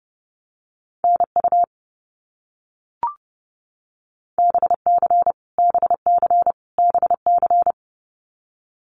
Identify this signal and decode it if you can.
Morse code, heard as a keyed tone, about 30 wpm, 700 Hz. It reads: NU BC BC BC
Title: Canadian Provinces and Territories - Repeated 3x - 30wpm